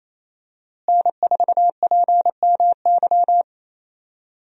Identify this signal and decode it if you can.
Morse code, heard as a keyed tone, about 28 wpm, 700 Hz. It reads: N4PMY
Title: USA Call Signs - 5 Characters - Repeated 2x - 28wpm